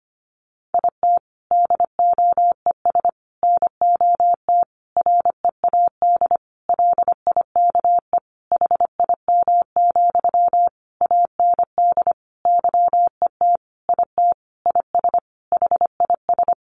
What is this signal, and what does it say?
Morse code, heard as a keyed tone, about 25 wpm, 700 Hz. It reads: IT DOES NOT READ LIKE HIM, AND YET IT IS HIS
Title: Sentences from Top 800 Words 25wpm